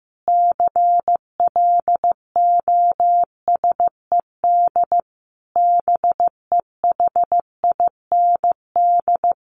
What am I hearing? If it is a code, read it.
Morse code, heard as a keyed tone, about 15 wpm, 700 Hz. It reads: CLOSED BEHIND
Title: Sentences from Top 1400 Words 15wpm